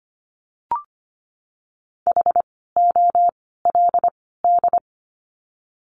Morse code, heard as a keyed tone, about 25 wpm, 700 Hz. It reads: H O L D
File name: course / 25:10wpm / 094 Lesson 16f - Letter c - Words 25wpm